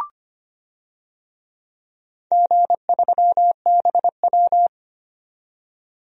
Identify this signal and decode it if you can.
Morse code, heard as a keyed tone, about 25 wpm, 700 Hz. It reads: G3BW
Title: DXCC-100 Call Signs - 4 Characters - 25wpm